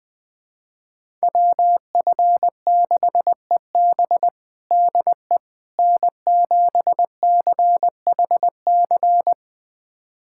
Morse code, heard as a keyed tone, about 20 wpm, 700 Hz. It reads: WF6EB DE N7CHC
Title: USA Callsign DE Callsign - 5 Characters - 20wpm